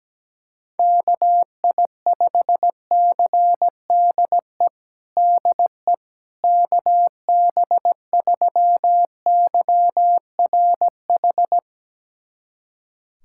Morse code, heard as a keyed tone, about 17 wpm, 700 Hz. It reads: KI5CDE DE KB3YRH
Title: USA Callsign DE Callsign 17wpm